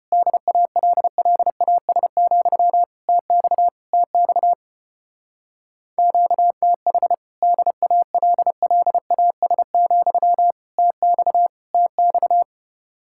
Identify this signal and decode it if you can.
Morse code, heard as a keyed tone, about 33 wpm, 700 Hz. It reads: DALLAS, TX TX QTH DALLAS, TX TX